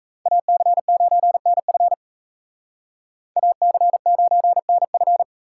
Morse code, heard as a keyed tone, about 36 wpm, 700 Hz. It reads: AC9NF AC9NF